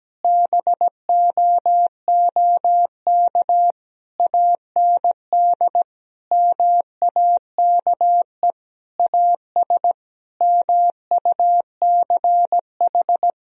Morse code, heard as a keyed tone, about 17 wpm, 700 Hz. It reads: BOOK AND MAKE AS MUCH
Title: Sentences from Top 1000 Words - Review Part 3 - 17wpm